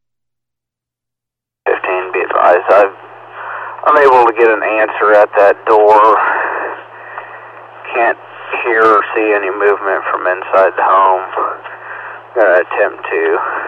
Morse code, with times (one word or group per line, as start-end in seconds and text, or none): none